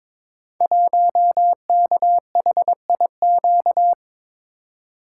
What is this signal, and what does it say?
Morse code, heard as a keyed tone, about 22 wpm, 700 Hz. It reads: 1KHIQ
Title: Code Groups - 5 Letter:Numbers - 22wpm